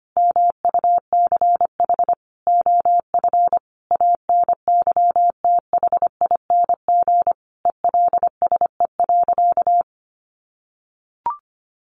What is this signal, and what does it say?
Morse code, heard as a keyed tone, about 25 wpm, 700 Hz. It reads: MUCH OF ANYTHING ELSE.
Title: Sentences from Top 1500 Words 25wpm